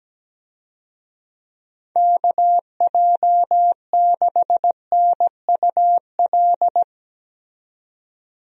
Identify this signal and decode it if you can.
Morse code, heard as a keyed tone, about 17 wpm, 700 Hz. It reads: KJ6NUL
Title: USA Call Signs - 6 Characters - 17wpm